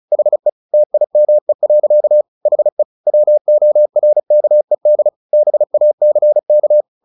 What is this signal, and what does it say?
Morse code, heard as a keyed tone, about 34 wpm, 600 Hz. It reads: HE TIME. HE WORKED BACK
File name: the old man and the sea 2 35wpm